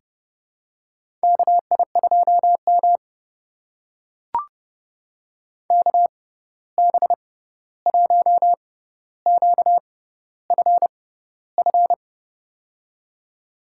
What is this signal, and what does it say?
Morse code, heard as a keyed tone, about 30 wpm, spaced as for 14 wpm, 700 Hz. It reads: KI2M KB1QFF
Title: USA Call Signs - 6x Character Spacing - 30wpm